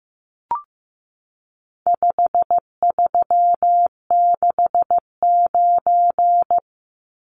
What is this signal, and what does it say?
Morse code, heard as a keyed tone, about 14 wpm, 700 Hz. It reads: 5369
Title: Numbers - 4-Digit 15wpm